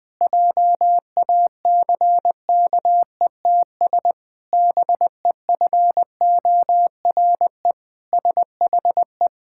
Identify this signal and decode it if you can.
Morse code, heard as a keyed tone, about 21 wpm, 700 Hz. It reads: JACKETS BEFORE SHE
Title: Pangram Challenge v2 - 20wpm